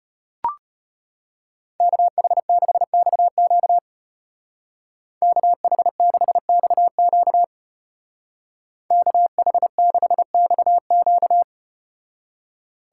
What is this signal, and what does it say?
Morse code, heard as a keyed tone, about 35 wpm, 700 Hz. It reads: KH6XQ KH6XQ KH6XQ